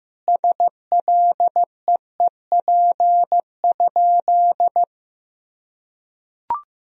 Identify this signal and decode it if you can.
Morse code, heard as a keyed tone, about 15 wpm, 700 Hz. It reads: SLEEP?
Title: Sentences from Top 500 Words and Abbreviations 15wpm